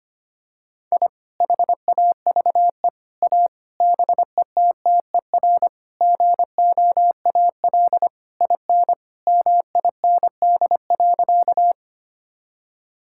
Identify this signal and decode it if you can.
Morse code, heard as a keyed tone, about 25 wpm, 700 Hz. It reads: I HAVE A BETTER GOAL IN MIND.
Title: Sentences from Top 1000 Words - Review Part 5 - 25wpm